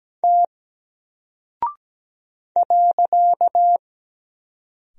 Morse code, heard as a keyed tone, about 17 wpm, 700 Hz. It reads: T .
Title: Single Character (Letter:Number:Punctuation:Prosign) Alt 17wpm